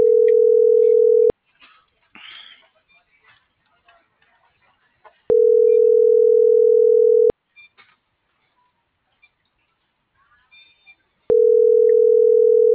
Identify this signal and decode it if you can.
Ringback tone (North America), 440 + 480 Hz.